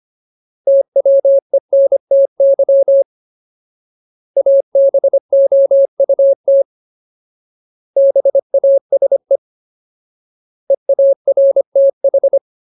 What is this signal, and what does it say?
Morse code, heard as a keyed tone, about 25 wpm, 550 Hz. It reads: TWENTY ABOUT BASE EARTH
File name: Ultimate Word List / 25WPM Ultimate